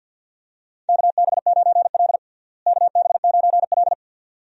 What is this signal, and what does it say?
Morse code, heard as a keyed tone, about 51 wpm, 700 Hz. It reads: KB9L KB9L
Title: USA Call Signs - 4 Characters - Repeated 2x - 50wpm